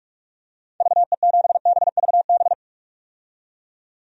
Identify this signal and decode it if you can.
Morse code, heard as a keyed tone, about 46 wpm, 700 Hz. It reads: VE7BVB
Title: DXCC-100 Call Signs - 6 Characters - 45wpm